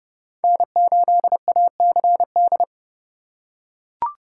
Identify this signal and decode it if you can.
Morse code, heard as a keyed tone, about 30 wpm, 700 Hz. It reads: N8ACD
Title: USA Call Signs - 3x Character Spacing - 30wpm